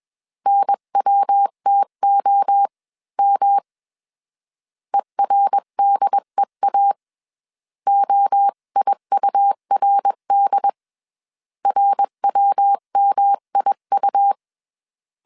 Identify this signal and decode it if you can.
Morse code, heard as a keyed tone, about 21 wpm, 800 Hz. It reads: NWTOM ERDEA OIURD RWMIU